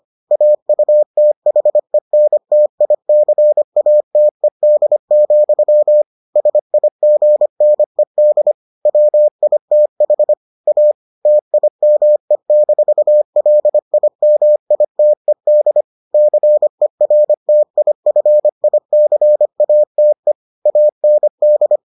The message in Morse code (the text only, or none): AUTHENTICATED, SIGNED WITH A TIME-LIMITED CERTIFICATE AND